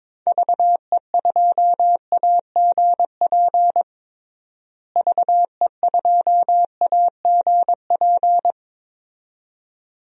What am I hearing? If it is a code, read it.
Morse code, heard as a keyed tone, about 21 wpm, 700 Hz. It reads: VE2AGP VE2AGP
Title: DXCC-100 Call Signs - 6 Characters - Repeated 2x - 22wpm